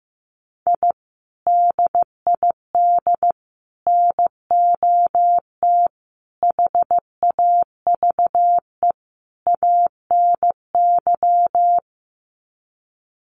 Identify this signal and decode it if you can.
Morse code, heard as a keyed tone, about 15 wpm, 700 Hz. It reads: I DID NOT HAVE ANY
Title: Sets of 5 Words Encore 15wpm